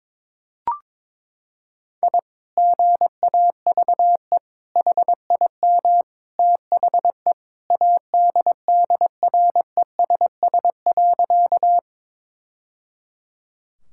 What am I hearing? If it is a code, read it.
Morse code, heard as a keyed tone, about 22 wpm, 700 Hz. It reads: I GAVE HIM THE ADDRESS.
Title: Sentences from Top 1000 Words - Review Part 6 - 22wpm